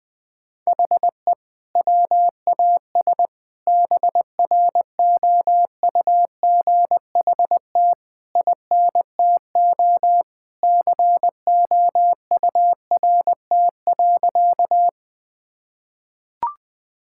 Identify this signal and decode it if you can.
Morse code, heard as a keyed tone, about 21 wpm, 700 Hz. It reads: HE WAS BROUGHT INTO COURT.
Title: Sentences from Top 1600 Words 20wpm